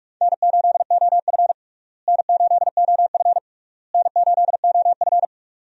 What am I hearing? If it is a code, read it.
Morse code, heard as a keyed tone, about 43 wpm, 700 Hz. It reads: N8OF N8OF N8OF
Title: USA Call Signs - 4 Characters - Repeated 3x - 45wpm